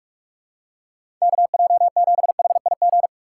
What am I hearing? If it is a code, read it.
Morse code, heard as a keyed tone, about 46 wpm, 700 Hz. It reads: KJ7HIG
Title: USA Call Signs - 6 Characters - 2x Character Spacing - 45wpm